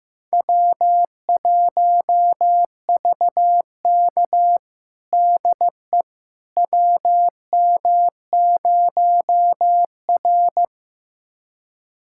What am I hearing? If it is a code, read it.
Morse code, heard as a keyed tone, about 15 wpm, 700 Hz. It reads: W1VK DE WM0R
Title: USA Callsign DE Callsign - 4 Characters - 15wpm